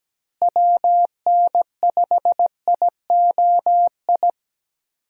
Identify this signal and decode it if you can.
Morse code, heard as a keyed tone, about 17 wpm, 700 Hz. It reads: WN5IOI
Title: USA Call Signs - 6 Characters - Repeated 2x - 17wpm